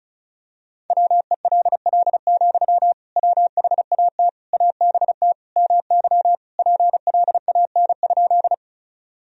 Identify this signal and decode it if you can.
Morse code, heard as a keyed tone, about 36 wpm, 700 Hz. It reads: WELL, WHAT ABT MY PLAN?
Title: Sentences from Top 500 Words and Abbreviations 35wpm